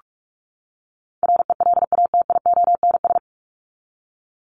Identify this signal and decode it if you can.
Morse code, heard as a keyed tone, about 44 wpm, 700 Hz. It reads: RELATIONS
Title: ES:S Suffix - Top 100 Words with Base Words 45wpm